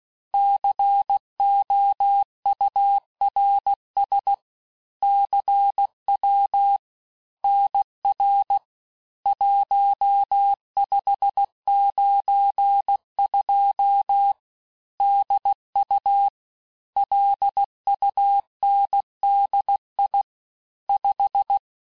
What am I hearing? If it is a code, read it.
Morse code, heard as a keyed tone, about 16 wpm, 800 Hz. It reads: COURS CW NR 1592 DU LUNDI 5
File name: cours-du-lundi files / cours-1592--teste-chiffre-vitesse-15mpm